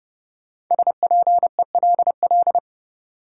Morse code, heard as a keyed tone, about 32 wpm, 700 Hz. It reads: SPELL
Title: Sentences from Top 1000 Words - Review Part 1 - New Word Drill - 30wpm